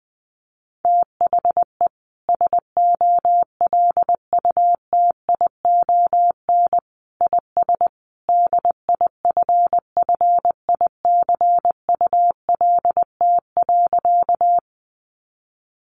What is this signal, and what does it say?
Morse code, heard as a keyed tone, about 20 wpm, 700 Hz. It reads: THE SOLUTION IS DIFFICULT.
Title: Sentences from Top 1300 Words 20wpm